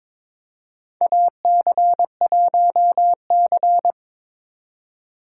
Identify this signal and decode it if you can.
Morse code, heard as a keyed tone, about 22 wpm, 700 Hz. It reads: AC1C
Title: USA Call Signs - 2x Character Spacing - 22wpm